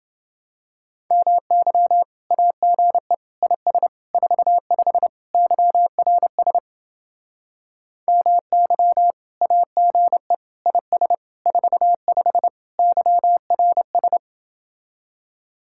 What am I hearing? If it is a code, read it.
Morse code, heard as a keyed tone, about 29 wpm, 700 Hz. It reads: MY AGE IS 45 YRS MY AGE IS 45 YRS